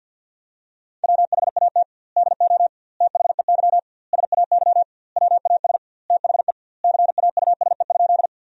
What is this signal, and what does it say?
Morse code, heard as a keyed tone, about 51 wpm, 700 Hz. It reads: WHAT DO THEY SAY WAS THE CAUSE?
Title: 2x Word-Spacing - Sentences from Top 200 Words - 50wpm